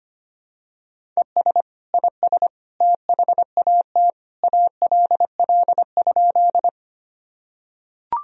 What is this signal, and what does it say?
Morse code, heard as a keyed tone, about 25 wpm, 700 Hz. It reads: ES IS THAT ALL?
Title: Sentences from Top 100 Words and Abbreviations 25wpm